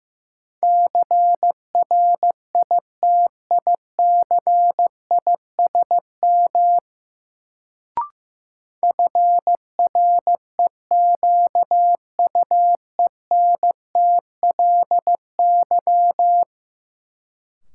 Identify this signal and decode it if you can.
Morse code, heard as a keyed tone, about 15 wpm, 700 Hz. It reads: CRITICISM FREQUENTLY
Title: Sentences from Top 1300 Words - New Word Drill - 15wpm